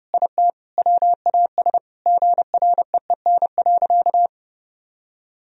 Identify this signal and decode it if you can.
Morse code, heard as a keyed tone, about 30 wpm, 700 Hz. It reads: IT WAS GREEN.
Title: Sentences from Top 1000 Words - Review Part 2 - 30wpm